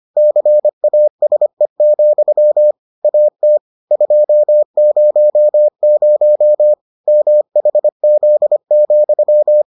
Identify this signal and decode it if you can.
Morse code, heard as a keyed tone, about 25 wpm, 600 Hz. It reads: CASE, AT 200 MHZ,